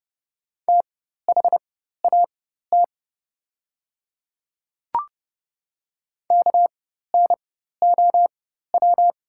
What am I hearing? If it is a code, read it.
Morse code, heard as a keyed tone, about 30 wpm, spaced as for 14 wpm, 700 Hz. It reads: THAT KNOW
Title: 4x Character Spacing - Top 100 Words - 30wpm